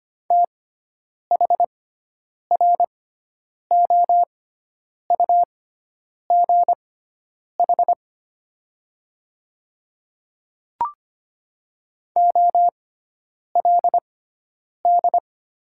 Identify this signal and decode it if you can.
Morse code, heard as a keyed tone, about 25 wpm, spaced as for 9 wpm, 700 Hz. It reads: THROUGH OLD
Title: 6x Character Spacing - Top 300 Words - 25wpm